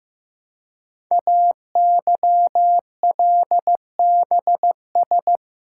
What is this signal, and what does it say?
Morse code, heard as a keyed tone, about 15 wpm, 700 Hz. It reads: AYLBS
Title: Code Groups - 5 Letter:Numbers - 15wpm